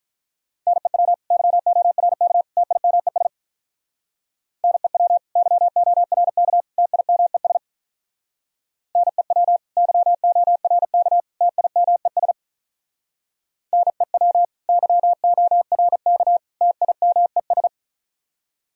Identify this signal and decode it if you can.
Morse code, heard as a keyed tone, about 44 wpm, 700 Hz. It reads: NEW YORK TIMES NEW YORK TIMES NEW YORK TIMES NEW YORK TIMES
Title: Speed-Racing - Sets of 3 Words 35wpm